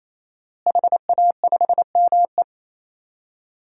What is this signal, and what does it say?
Morse code, heard as a keyed tone, about 28 wpm, 700 Hz. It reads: HA5ME
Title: DXCC-100 Call Signs - 5 Characters - Repeated 3x - 28wpm